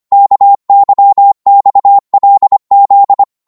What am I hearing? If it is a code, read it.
Morse code, heard as a keyed tone, about 25 wpm, 800 Hz. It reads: KYXLZ
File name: Morse / ltr-25wpm